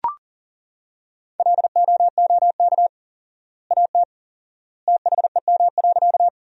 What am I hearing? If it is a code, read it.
Morse code, heard as a keyed tone, about 40 wpm, 700 Hz. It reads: LOOK AT THEM.
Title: 4x Word-Spacing - Sentences from Top 100 Words - 40wpm